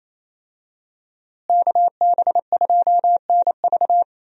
Morse code, heard as a keyed tone, about 27 wpm, 700 Hz. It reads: KB2NV
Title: USA Call Signs - 5 Characters - Repeated 2x - 28wpm